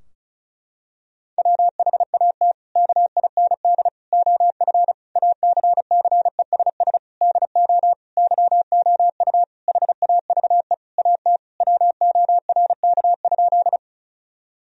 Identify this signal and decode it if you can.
Morse code, heard as a keyed tone, about 36 wpm, 700 Hz. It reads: WHAT KIND OF ACCESS DO YOU HAVE AT WORK?
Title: Sentences from Top 1000 Words - Review Part 4 - 35wpm